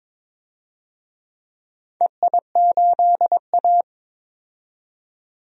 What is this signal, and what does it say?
Morse code, heard as a keyed tone, about 22 wpm, 700 Hz. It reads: EI8A
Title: DXCC-100 Call Signs - 4 Characters - 22wpm